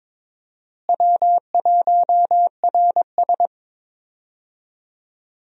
Morse code, heard as a keyed tone, about 22 wpm, 700 Hz. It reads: W1RS
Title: USA Call Signs - 4 Characters - Repeated 3x - 22wpm